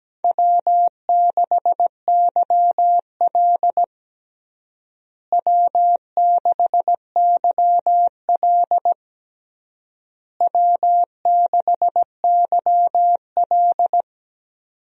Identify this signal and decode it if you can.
Morse code, heard as a keyed tone, about 17 wpm, 700 Hz. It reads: W6YL W6YL W6YL
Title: USA Call Signs - 4 Characters - Repeated 3x - 17wpm